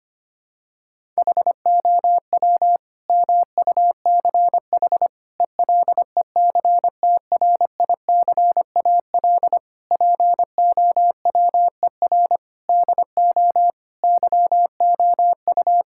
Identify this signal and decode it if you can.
Morse code, heard as a keyed tone, about 25 wpm, 700 Hz. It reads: HOW MUCH ELECTRICAL POWER DO YOU
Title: Sentences from Top 1000 Words - Review Part 6 - 25wpm